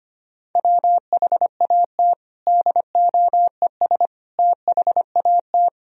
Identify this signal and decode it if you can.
Morse code, heard as a keyed tone, about 25 wpm, 700 Hz. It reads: WHAT DOES THAT
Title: Sentences from Top 1000 Words - Review Part 1 - 25wpm